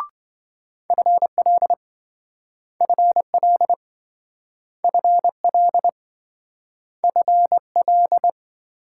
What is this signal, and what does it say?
Morse code, heard as a keyed tone, about 27 wpm, 700 Hz. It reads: FL FL FL FL